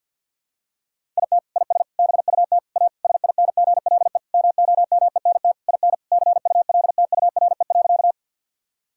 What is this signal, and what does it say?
Morse code, heard as a keyed tone, about 50 wpm, 700 Hz. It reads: IT IS BUT A SINGLE MOMENT IN CULTURE.
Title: Sentences from Top 1000 Words - Review Part 6 - 50wpm